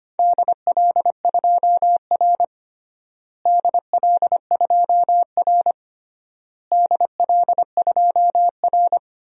Morse code, heard as a keyed tone, about 25 wpm, 700 Hz. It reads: DL2R DL2R DL2R
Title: DXCC-100 Call Signs - 4 Characters - Repeated 3x - 25wpm